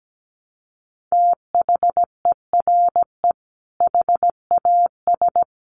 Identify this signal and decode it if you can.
Morse code, heard as a keyed tone, about 17 wpm, 700 Hz. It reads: THERE HAS
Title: Sentences from Top 700 Words 17wpm